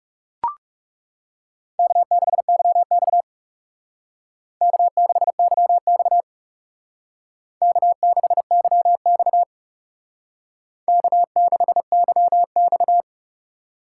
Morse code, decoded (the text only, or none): K6YX K6YX K6YX K6YX